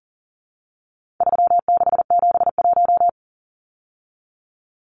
Morse code, heard as a keyed tone, about 39 wpm, 700 Hz. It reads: 3671
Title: Numbers - 4-Digit 40wpm